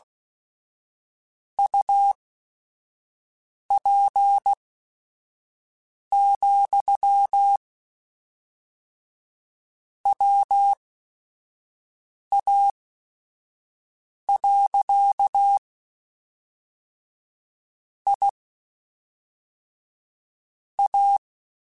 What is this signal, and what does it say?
Morse code, heard as a keyed tone, about 16 wpm, spaced as for 4 wpm, 800 Hz. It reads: UP, WA. IA